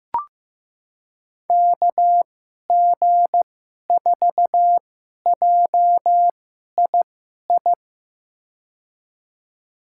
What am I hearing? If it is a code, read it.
Morse code, heard as a keyed tone, about 15 wpm, spaced as for 11 wpm, 700 Hz. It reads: KG4JII
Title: USA Call Signs - 6 Characters - 2x Character Spacing - 15wpm